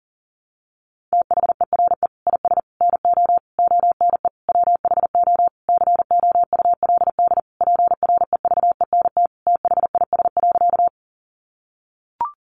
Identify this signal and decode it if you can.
Morse code, heard as a keyed tone, about 40 wpm, 700 Hz. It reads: THERE IS NO ONE WHO COULD PREVENT THIS.
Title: Sentences from Top 1100 Words 40wpm